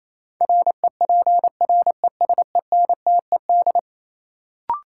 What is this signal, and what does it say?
Morse code, heard as a keyed tone, about 28 wpm, 700 Hz. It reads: REPRESENTED
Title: Sets of 5 Words - New Word Drill - 28wpm